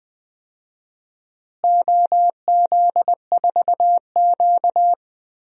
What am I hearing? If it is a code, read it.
Morse code, heard as a keyed tone, about 20 wpm, 700 Hz. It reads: OZ4Q
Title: DXCC-100 Call Signs - 4 Characters - Repeated 2x - 20wpm